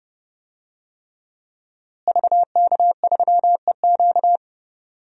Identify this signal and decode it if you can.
Morse code, heard as a keyed tone, about 32 wpm, 700 Hz. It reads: VK3EQ
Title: DXCC-100 Call Signs - 5 Characters - Repeated 3x - 30wpm